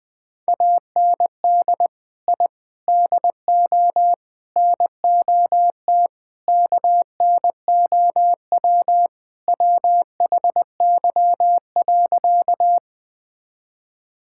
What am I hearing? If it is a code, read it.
Morse code, heard as a keyed tone, about 20 wpm, 700 Hz. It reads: AND I DO NOT KNOW WHY.
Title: Sentences with 5-Word Phrases - 20wpm